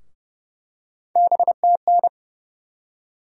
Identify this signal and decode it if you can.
Morse code, heard as a keyed tone, about 30 wpm, 700 Hz. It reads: BTN
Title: USA Call Sign Postfixes - 3 Letters 30wpm